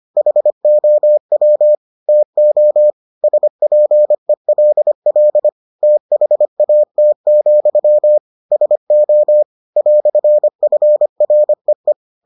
Morse code, heard as a keyed tone, about 25 wpm, 600 Hz. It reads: HOW TO SPELL THAT, SO "FREE